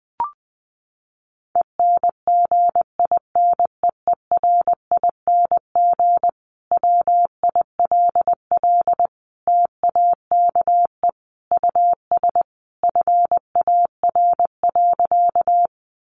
Morse code, heard as a keyed tone, about 20 wpm, 700 Hz. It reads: ENGINEERING WILL TAKE US FAR.
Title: Sentences from Top 1500 Words 20wpm